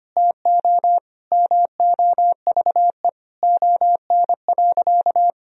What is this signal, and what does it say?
Morse code, heard as a keyed tone, about 25 wpm, 700 Hz. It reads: TO MOVE ON.
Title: Sentences from Top 1000 Words - Review Part 1 - 25wpm